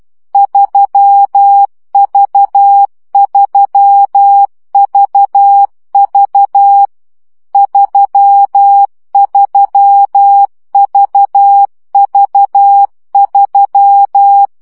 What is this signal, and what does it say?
Morse code, heard as a keyed tone, about 12 wpm, 800 Hz. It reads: 3V3VV 33VV3
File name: CW gyakorlas v 3 10 baud